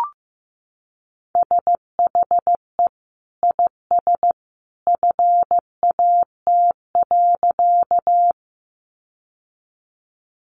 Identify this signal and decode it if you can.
Morse code, heard as a keyed tone, about 15 wpm, 700 Hz. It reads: SHE IS FAT.
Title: Sentences from Top 600 Words 15wpm